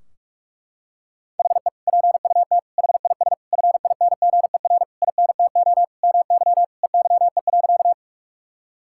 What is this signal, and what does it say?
Morse code, heard as a keyed tone, about 41 wpm, 700 Hz. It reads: HE PUT HIS FINGER INTO MY EYE.
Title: Sentences from Top 1600 Words 45wpm